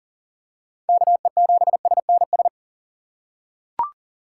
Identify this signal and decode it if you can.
Morse code, heard as a keyed tone, about 40 wpm, 690 Hz. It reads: KE7SNS
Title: USA Call Signs - 3x Character Spacing - 40wpm